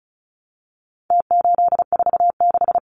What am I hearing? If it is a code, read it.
Morse code, heard as a keyed tone, about 34 wpm, 700 Hz. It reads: T846
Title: Numbers - Contest Style - 4-Digit 35wpm